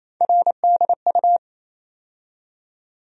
Morse code, heard as a keyed tone, about 28 wpm, 700 Hz. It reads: RDU
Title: USA Call Sign Postfixes - 3 Letters 28wpm